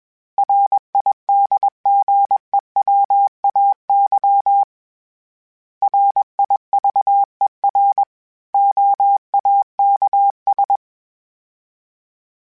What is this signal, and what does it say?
Morse code, heard as a keyed tone, about 21 wpm, 800 Hz. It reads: RIDGEWAY RIVER OAKS
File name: fromSMs / FromSMs mp3s / Cities@20